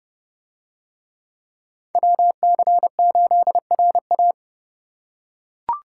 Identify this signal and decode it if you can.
Morse code, heard as a keyed tone, about 28 wpm, 700 Hz. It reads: WC8RA